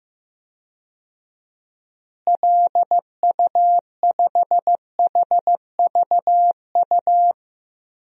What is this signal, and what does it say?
Morse code, heard as a keyed tone, about 15 wpm, 700 Hz. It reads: LU5HVU
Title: DXCC-100 Call Signs - 6 Characters - Repeated 2x - 15wpm